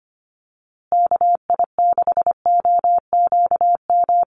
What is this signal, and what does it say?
Morse code, heard as a keyed tone, about 25 wpm, 700 Hz. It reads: KI6OQM
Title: USA Call Signs Repeated 3x 25wpm